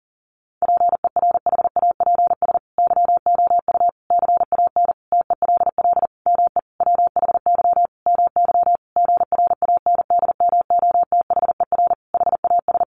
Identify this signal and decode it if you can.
Morse code, heard as a keyed tone, about 40 wpm, 700 Hz. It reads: PERHAPS YOU CAN TELL ME WHY MY GRANDMOTHER HAS
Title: Sentences from Top 1200 Words 40wpm